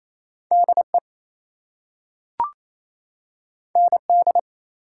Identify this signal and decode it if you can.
Morse code, heard as a keyed tone, about 28 wpm, 700 Hz. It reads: DE ND
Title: US States and Canadian Provinces 28wpm